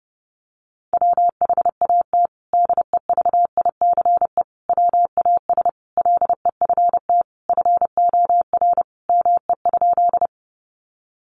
Morse code, heard as a keyed tone, about 30 wpm, 700 Hz. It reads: WHAT DEVICE WAS LEFT FOR ME?
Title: Sentences from Top 1500 Words 30wpm